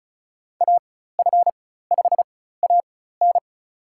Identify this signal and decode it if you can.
Morse code, heard as a keyed tone, about 36 wpm, spaced as for 16 wpm, 700 Hz. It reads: AF5AN
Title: USA Call Signs - 5 Characters - 4x Character Spacing - 35wpm